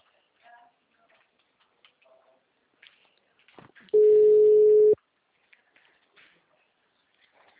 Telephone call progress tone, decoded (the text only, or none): ringback tone